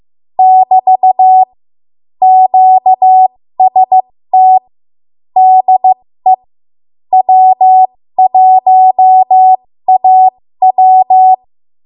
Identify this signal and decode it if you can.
Morse code, heard as a keyed tone, about 15 wpm, spaced as for 13 wpm, 750 Hz. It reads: <BT> QST DE W1AW
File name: Morse / 13WPM